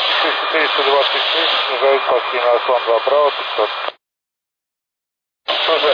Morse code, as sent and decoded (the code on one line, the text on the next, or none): none